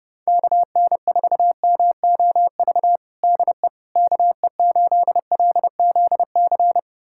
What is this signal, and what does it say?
Morse code, heard as a keyed tone, about 30 wpm, 700 Hz. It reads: KN4MOV DE KE8LZC
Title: USA Callsign DE Callsign - 6 Characters - 30wpm